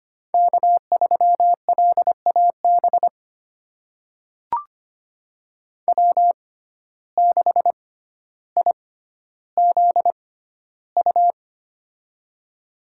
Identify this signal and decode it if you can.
Morse code, heard as a keyed tone, about 25 wpm, spaced as for 13 wpm, 700 Hz. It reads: K3LAB W6IZU
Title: USA Call Signs - 5 Characters - 6x Character Spacing - 25wpm